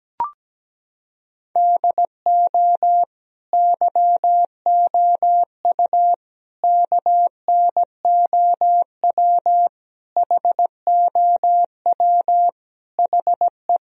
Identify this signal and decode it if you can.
Morse code, heard as a keyed tone, about 17 wpm, 700 Hz. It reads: DO YOU KNOW HOW HE
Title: Sentences from Top 1000 Words - Review Part 1 - 17wpm